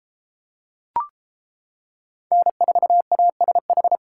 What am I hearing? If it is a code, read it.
Morse code, heard as a keyed tone, about 33 wpm, 700 Hz. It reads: N4ASH